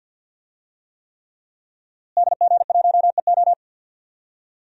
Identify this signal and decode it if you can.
Morse code, heard as a keyed tone, about 50 wpm, 700 Hz. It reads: DG1EO
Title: DXCC-100 Call Signs - 5 Characters - Repeated 3x - 50wpm